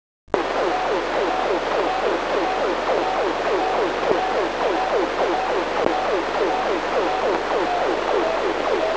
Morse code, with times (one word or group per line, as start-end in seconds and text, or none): none